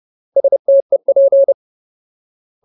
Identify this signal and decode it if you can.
Morse code, heard as a keyed tone, about 30 wpm, 550 Hz. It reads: STEP
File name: Ultimate Word List / 30WPM Ultimate